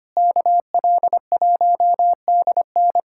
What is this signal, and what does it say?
Morse code, heard as a keyed tone, about 25 wpm, 700 Hz. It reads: KL1DN
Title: USA Callsign DE Callsign - 5 Characters - 25wpm